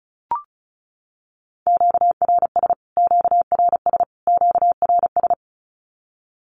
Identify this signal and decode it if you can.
Morse code, heard as a keyed tone, about 34 wpm, 700 Hz. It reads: QRS QRS QRS
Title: Q-Codes Repeated 3x 35wpm